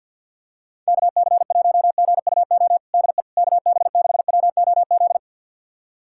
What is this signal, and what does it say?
Morse code, heard as a keyed tone, about 51 wpm, 700 Hz. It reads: KC1GUO DE KB6WOZ
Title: USA Callsign DE Callsign 50wpm